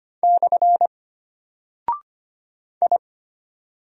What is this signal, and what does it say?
Morse code, heard as a keyed tone, about 25 wpm, 700 Hz. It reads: / I
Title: Single Character (Letter:Number:Punctuation:Prosign) Alt 25wpm